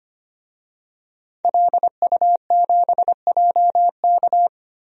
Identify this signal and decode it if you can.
Morse code, heard as a keyed tone, about 25 wpm, 700 Hz. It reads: LU7JK
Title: DXCC-100 Call Signs - 5 Characters - Repeated 2x - 25wpm